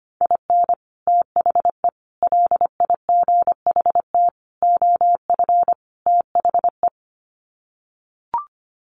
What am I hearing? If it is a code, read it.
Morse code, heard as a keyed tone, about 25 wpm, 700 Hz. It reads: IN THE LIGHT OF THE
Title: Sets of 5 Words Encore2 25wpm